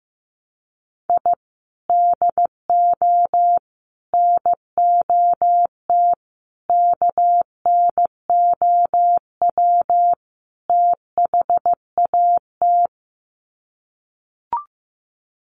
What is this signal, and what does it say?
Morse code, heard as a keyed tone, about 15 wpm, 700 Hz. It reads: I DO NOT KNOW THAT
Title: Sets of 5 Words 15wpm